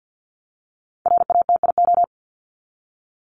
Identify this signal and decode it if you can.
Morse code, heard as a keyed tone, about 50 wpm, 700 Hz. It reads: RATIO
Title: Top 1751-2000 Words 50wpm